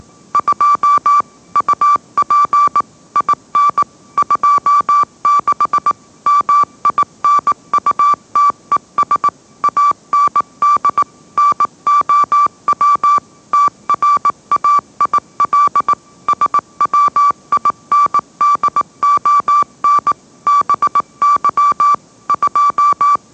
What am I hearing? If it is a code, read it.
Morse code, heard as a keyed tone, about 18 wpm, 1160 Hz. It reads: 2 UP IN 26 MINUTES AND NOW TRAIL SWINDON BY 2
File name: mrx / mrx081 20wpm